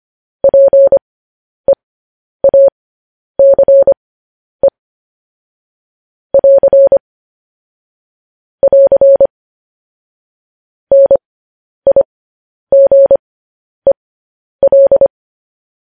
Morse code, heard as a keyed tone, about 25 wpm, spaced as for 10 wpm, 550 Hz. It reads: PEACE <AR> <AR> NIGEL
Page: news-headlines-in-morse-code-at-10-wpm